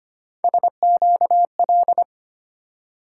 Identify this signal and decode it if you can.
Morse code, heard as a keyed tone, about 25 wpm, 700 Hz. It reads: SQL
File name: Three Letter-Number Combinations - 25wpm